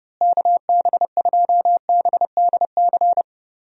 Morse code, heard as a keyed tone, about 30 wpm, 700 Hz. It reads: KB2BDC
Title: USA Call Signs - 6 Characters - Repeated 2x - 30wpm